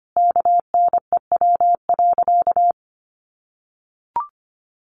Morse code, heard as a keyed tone, about 25 wpm, 700 Hz. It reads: KNEW.
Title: Sentences from Top 500 Words 25wpm